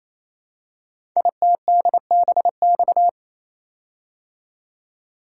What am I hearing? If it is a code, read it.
Morse code, heard as a keyed tone, about 27 wpm, 700 Hz. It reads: ITDBX
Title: Code Groups - 5 Letters - 28wpm